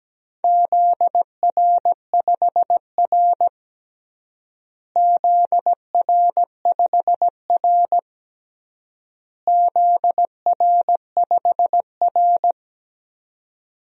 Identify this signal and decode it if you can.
Morse code, heard as a keyed tone, about 17 wpm, 700 Hz. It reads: ZR5R ZR5R ZR5R
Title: DXCC-100 Call Signs - 4 Characters - Repeated 3x - 17wpm